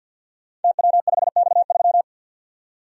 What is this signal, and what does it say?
Morse code, heard as a keyed tone, about 47 wpm, 700 Hz. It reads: TW5X3
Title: Code Groups - 5 Letter:Numbers - 50wpm